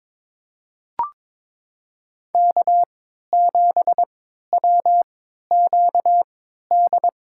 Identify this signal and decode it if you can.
Morse code, heard as a keyed tone, about 23 wpm, spaced as for 13 wpm, 700 Hz. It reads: K7WQD
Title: USA Call Signs - 5 Characters - 3x Character Spacing - 22wpm